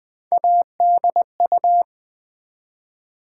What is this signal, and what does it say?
Morse code, heard as a keyed tone, about 20 wpm, 700 Hz. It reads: ADU
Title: Build A Word - 1 Letter At A Time - 3 to 5-Letter Words - 20wpm